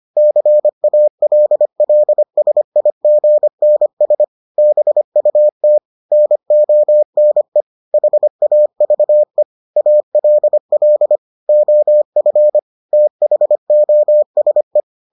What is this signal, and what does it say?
Morse code, heard as a keyed tone, about 25 wpm, 600 Hz. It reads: CALLSIGNS BUT NONE HAVE ALL OF THOSE